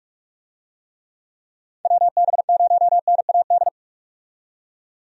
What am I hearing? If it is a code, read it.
Morse code, heard as a keyed tone, about 43 wpm, 700 Hz. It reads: WB0NAD
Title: USA Call Signs - 6 Characters - 45wpm